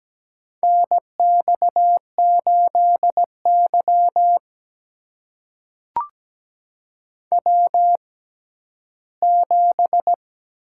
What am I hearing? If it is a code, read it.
Morse code, heard as a keyed tone, about 17 wpm, spaced as for 11 wpm, 700 Hz. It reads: NX8Y W7
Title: USA Call Signs - 6x Character Spacing - 17wpm